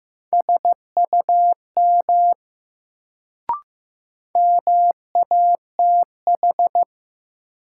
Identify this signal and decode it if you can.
Morse code, heard as a keyed tone, about 15 wpm, 700 Hz. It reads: SUM MATH
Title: Sentences from Top 1000 Words - Review Part 6 - New Word Drill - 15wpm